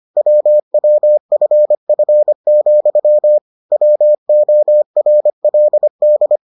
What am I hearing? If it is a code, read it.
Morse code, heard as a keyed tone, about 25 wpm, 600 Hz. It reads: WWFF, WORLD